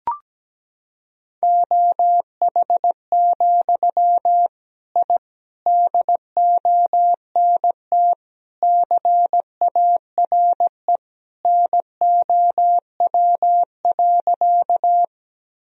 Morse code, heard as a keyed tone, about 17 wpm, 700 Hz. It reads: OH, I DONT CARE NOW.
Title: Sentences from Top 1000 Words - Review Part 1 - 17wpm